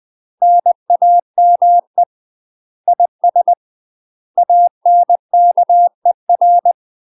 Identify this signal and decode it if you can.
Morse code, heard as a keyed tone, about 20 wpm, 700 Hz. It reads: NAME IS ANKER
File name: UK MASTER QSOs TRAINING / 20WPM / UK MASTER QSOs 0005